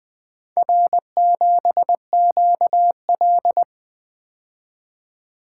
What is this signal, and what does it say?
Morse code, heard as a keyed tone, about 23 wpm, 700 Hz. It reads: R7QL
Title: Speed-Racing - Four Letter-Number Combinations - 20wpm